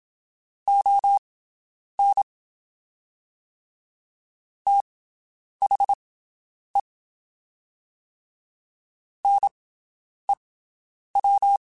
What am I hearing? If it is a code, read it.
Morse code, heard as a keyed tone, about 27 wpm, spaced as for 9 wpm, 800 Hz. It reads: ON THE NEW
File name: QSOs Spaced